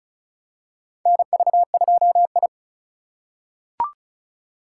Morse code, heard as a keyed tone, about 36 wpm, 700 Hz. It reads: NV2I
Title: USA Call Signs - 4 Characters - 2x Character Spacing - 35wpm